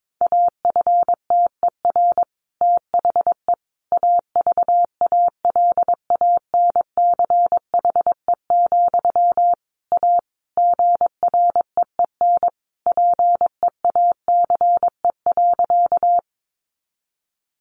Morse code, heard as a keyed tone, about 22 wpm, 700 Hz. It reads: AFTER THE AVALANCHE, A GREEN PEACE.
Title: A Sound Of Thunder 22wpm